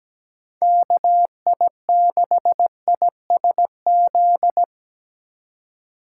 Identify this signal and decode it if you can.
Morse code, heard as a keyed tone, about 17 wpm, 700 Hz. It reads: KI6ISZ
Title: USA Call Signs - 3x Character Spacing - 17wpm